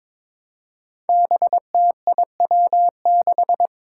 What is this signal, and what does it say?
Morse code, heard as a keyed tone, about 22 wpm, 700 Hz. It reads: BTIW6
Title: Five Challenging Letter-Number Combinations 22wpm